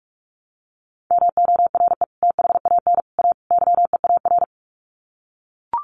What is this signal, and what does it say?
Morse code, heard as a keyed tone, about 44 wpm, 700 Hz. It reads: MORE THAN A YEAR
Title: Sets of 4 Words 45wpm